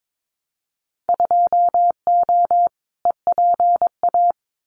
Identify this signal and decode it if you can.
Morse code, heard as a keyed tone, about 22 wpm, 700 Hz. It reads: 2O EPA
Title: Winter Field Day Exchanges Encore 22wpm